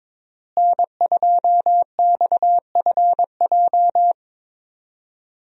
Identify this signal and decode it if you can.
Morse code, heard as a keyed tone, about 22 wpm, 700 Hz. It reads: N2XFJ
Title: USA Call Signs - 5 Characters - Repeated 3x - 22wpm